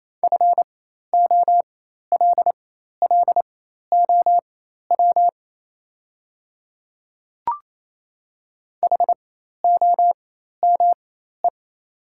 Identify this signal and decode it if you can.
Morse code, heard as a keyed tone, about 28 wpm, spaced as for 13 wpm, 700 Hz. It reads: FOLLOW HOME
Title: 4x Character Spacing - Top 200 Words - 28wpm